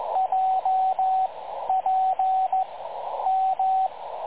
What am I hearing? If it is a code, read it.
Morse code, heard as a keyed tone, about 15 wpm, spaced as for 10 wpm, 750 Hz, with radio noise. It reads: JPM
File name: dlazaawansowanych / nagraniazpasma / websdr recording 3565.0kHz